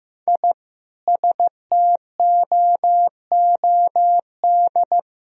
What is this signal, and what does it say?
Morse code, heard as a keyed tone, about 15 wpm, 700 Hz. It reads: I STOOD
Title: Sentences from Top 1000 Words - Review Part 2 - 15wpm